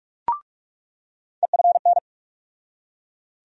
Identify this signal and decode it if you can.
Morse code, heard as a keyed tone, about 41 wpm, 700 Hz. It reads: EFN
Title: USA Call Sign Postfixes - 3 Letters 45wpm